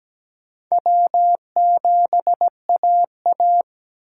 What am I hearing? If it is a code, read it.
Morse code, heard as a keyed tone, about 17 wpm, 700 Hz. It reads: W7AA
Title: USA Call Signs - 3x Character Spacing - 17wpm